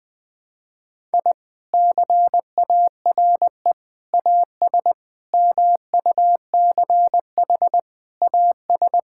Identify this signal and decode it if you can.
Morse code, heard as a keyed tone, about 20 wpm, 700 Hz. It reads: I CARE AS MUCH AS
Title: Sentences with 3-Word Phrases - 20wpm